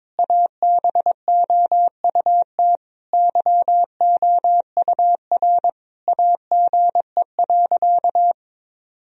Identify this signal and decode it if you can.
Morse code, heard as a keyed tone, about 21 wpm, 700 Hz. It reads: ABOUT YOUR AGE.
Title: Sentences from Top 1000 Words - Review Part 3 - 22wpm